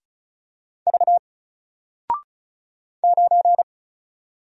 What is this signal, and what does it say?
Morse code, heard as a keyed tone, about 36 wpm, 700 Hz. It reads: V 9
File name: Single-Letter-Number-35wpm